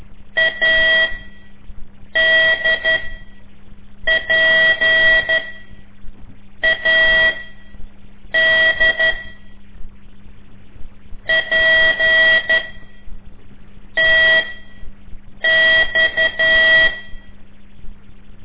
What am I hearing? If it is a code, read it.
Morse code, heard as a keyed tone, about 12 wpm, spaced as for 6 wpm, 1950 Hz. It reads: ADPAD PTX
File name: ik1weg2 / CORSI CW / lez5-1